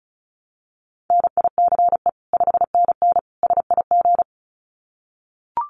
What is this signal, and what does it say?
Morse code, heard as a keyed tone, about 34 wpm, 700 Hz. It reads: NICE 5NN SIG
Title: Common QSO Phrases 35wpm